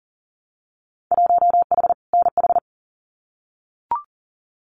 Morse code, heard as a keyed tone, about 40 wpm, 700 Hz. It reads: 1H NH
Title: Winter Field Day Exchanges 40wpm